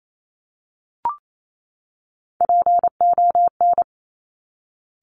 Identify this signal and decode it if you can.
Morse code, heard as a keyed tone, about 28 wpm, 700 Hz. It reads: PON
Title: Top Three Letter Combinations in Words 28wpm